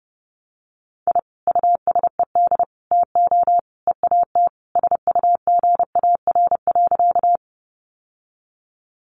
Morse code, heard as a keyed tone, about 30 wpm, 700 Hz. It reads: I USED TO EAT SUGAR.
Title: Sentences from Top 1200 Words 30wpm